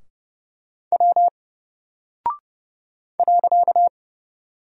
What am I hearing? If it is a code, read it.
Morse code, heard as a keyed tone, about 29 wpm, 700 Hz. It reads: W .
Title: Single Character (Letter:Number:Punctuation:Prosign) 30wpm